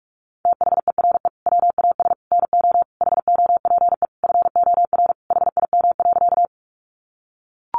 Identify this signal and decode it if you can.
Morse code, heard as a keyed tone, about 44 wpm, 700 Hz. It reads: THERE WAS NO HOPE FOR HIM.
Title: Sentences from Top 1000 Words 45wpm